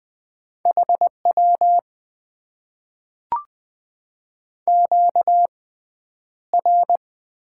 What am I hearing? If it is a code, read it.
Morse code, heard as a keyed tone, about 20 wpm, spaced as for 10 wpm, 700 Hz. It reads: HW QR
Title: Top 90 QSO Elements - 6x Character Spacing - 20wpm